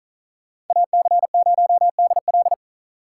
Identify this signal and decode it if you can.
Morse code, heard as a keyed tone, about 46 wpm, 700 Hz. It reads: AC0DL